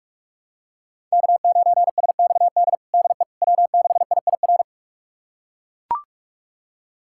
Morse code, heard as a keyed tone, about 42 wpm, 700 Hz. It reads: K9SXD DE W6IIR
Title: USA Callsign DE Callsign - 5 Characters - 45wpm